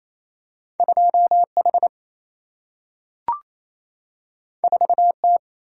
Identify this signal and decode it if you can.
Morse code, heard as a keyed tone, about 28 wpm, 700 Hz. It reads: 2H 4T
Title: Single-Letter Single-Number 28wpm